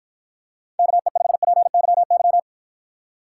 Morse code, heard as a keyed tone, about 53 wpm, 700 Hz. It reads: KE5PYY